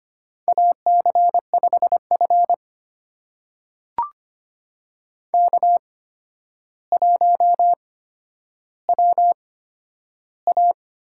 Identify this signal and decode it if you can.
Morse code, heard as a keyed tone, about 25 wpm, spaced as for 11 wpm, 700 Hz. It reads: AC5F K1WA
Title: USA Call Signs - 4 Characters - 8x Character Spacing - 25wpm